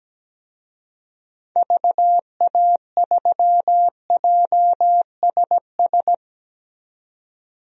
Morse code, heard as a keyed tone, about 17 wpm, 700 Hz. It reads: VA3JSS
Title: DXCC-100 Call Signs - 6 Characters - 17wpm